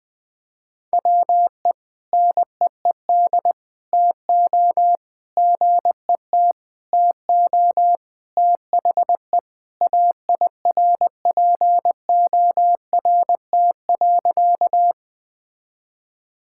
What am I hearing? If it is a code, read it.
Morse code, heard as a keyed tone, about 20 wpm, 700 Hz. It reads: WE NEED TO GET TO THE AIRPORT.
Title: Sentences with 3-Word Phrases - Spoken Beginning - 20wpm